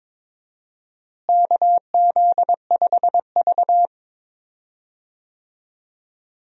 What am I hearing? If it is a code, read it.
Morse code, heard as a keyed tone, about 22 wpm, 700 Hz. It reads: KZ5V
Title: USA Call Signs - 4 Characters - Repeated 3x - 22wpm